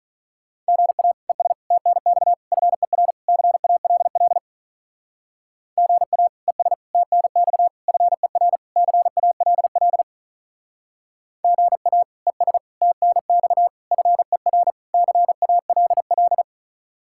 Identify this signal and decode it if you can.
Morse code, heard as a keyed tone, about 39 wpm, 700 Hz. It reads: GA ES TNX FER CALL GA ES TNX FER CALL GA ES TNX FER CALL
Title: Speed-Racing Common QSO Phrases - 35wpm